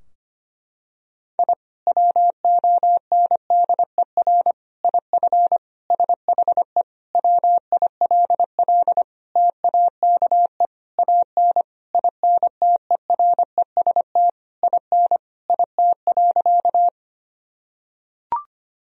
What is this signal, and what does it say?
Morse code, heard as a keyed tone, about 25 wpm, 700 Hz. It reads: I WONDER IF SHE WILL TAKE AN INTEREST IN IT.
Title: Sentences from Top 1000 Words - Review Part 3 - 25wpm